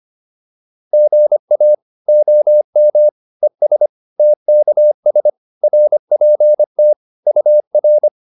Morse code, heard as a keyed tone, about 25 wpm, 600 Hz. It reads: GA OM ES TKS RPT UR